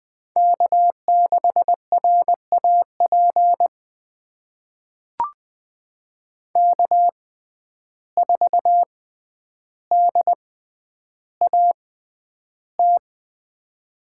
Morse code, heard as a keyed tone, about 20 wpm, spaced as for 10 wpm, 700 Hz. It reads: K6RAP K4DAT
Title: USA Call Signs - 5 Characters - 6x Character Spacing - 20wpm